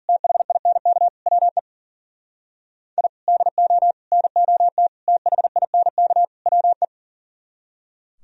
Morse code, heard as a keyed tone, about 50 wpm, 700 Hz. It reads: THINK WE I DO NOT THINK WE
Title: Speed-Racing - Sets of 5 Words 40wpm